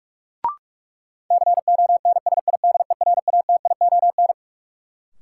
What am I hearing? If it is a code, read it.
Morse code, heard as a keyed tone, about 46 wpm, 700 Hz. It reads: CONSIDERATION
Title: Sentences from Top 1500 Words - New Word Drill Alt - 45wpm